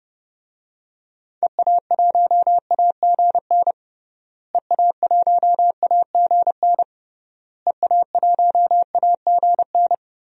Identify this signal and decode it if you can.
Morse code, heard as a keyed tone, about 30 wpm, 700 Hz. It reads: EA1AGN EA1AGN EA1AGN
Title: DXCC-100 Call Signs - 6 Characters - Repeated 3x - 30wpm